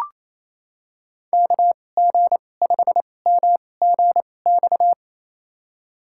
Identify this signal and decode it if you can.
Morse code, heard as a keyed tone, about 28 wpm, spaced as for 20 wpm, 700 Hz. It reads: KG5MGX
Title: USA Call Signs - 6 Characters - 2x Character Spacing - 28wpm